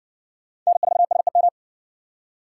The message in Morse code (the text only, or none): N4SEM